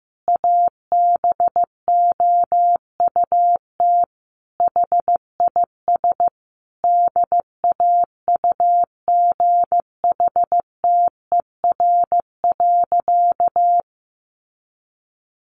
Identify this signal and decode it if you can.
Morse code, heard as a keyed tone, about 15 wpm, 700 Hz. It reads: ABOUT HIS DAUGHTER.
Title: Sentences from Top 1500 Words 15wpm